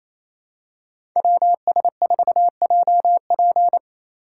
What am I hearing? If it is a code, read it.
Morse code, heard as a keyed tone, about 28 wpm, 700 Hz. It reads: WS4JP
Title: USA Call Signs - 5 Characters - 4x Character Spacing - 28wpm